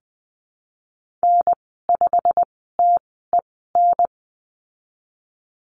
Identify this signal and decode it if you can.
Morse code, heard as a keyed tone, about 20 wpm, spaced as for 15 wpm, 700 Hz. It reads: N5TEN
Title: 122 Lesson 20g - Number 3 - Call Signs 20wpm